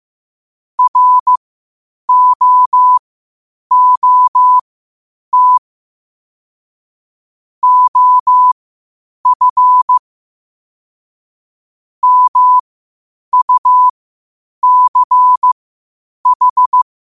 Morse code, heard as a keyed tone, about 15 wpm, spaced as for 8 wpm, 1000 Hz. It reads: ROOT OF MUCH
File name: qotdfiles / 8wpm